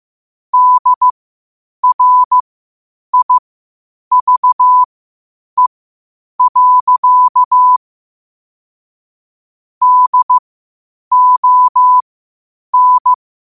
Morse code, heard as a keyed tone, about 15 wpm, spaced as for 8 wpm, 1000 Hz. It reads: DRIVE. DON
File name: 8wpm